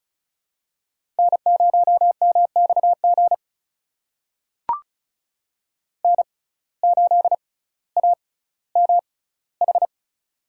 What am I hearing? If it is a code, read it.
Morse code, heard as a keyed tone, about 34 wpm, spaced as for 18 wpm, 700 Hz. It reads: N0MXG N8AMH
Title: USA Call Signs - 5 Characters - 6x Character Spacing - 35wpm